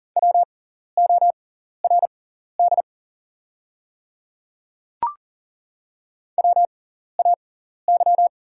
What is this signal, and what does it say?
Morse code, heard as a keyed tone, about 41 wpm, spaced as for 14 wpm, 700 Hz. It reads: WORD WAY
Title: 6x Character Spacing - Top 200 Words - 40wpm